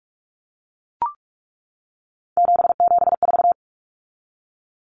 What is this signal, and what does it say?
Morse code, heard as a keyed tone, about 44 wpm, 700 Hz. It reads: 774
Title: Common Cut Numbers - 3-Digit 45wpm